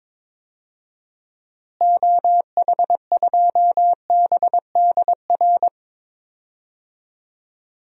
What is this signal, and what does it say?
Morse code, heard as a keyed tone, about 22 wpm, 700 Hz. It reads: OH2BDR
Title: DXCC-100 Call Signs - 6 Characters - Repeated 2x - 22wpm